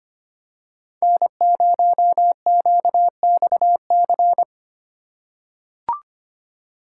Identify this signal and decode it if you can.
Morse code, heard as a keyed tone, about 25 wpm, 700 Hz. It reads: N0QXC
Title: USA Call Signs - 5 Characters - 3x Character Spacing - 25wpm